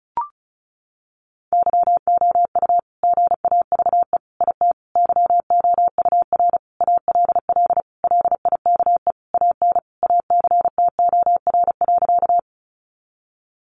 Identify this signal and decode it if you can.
Morse code, heard as a keyed tone, about 34 wpm, 700 Hz. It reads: YOU GAVE IT YOUR ALL LIKE AN ACTOR.
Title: Sentences from Top 1500 Words 35wpm